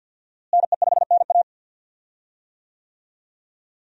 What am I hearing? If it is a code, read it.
Morse code, heard as a keyed tone, about 52 wpm, 700 Hz. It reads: NE5NA
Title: USA Call Signs - 5 Characters - Repeated 2x - 50wpm